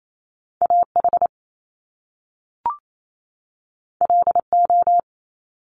Morse code, heard as a keyed tone, about 28 wpm, 700 Hz. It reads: AH LO
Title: Top Two Letter Combinations in Words 28wpm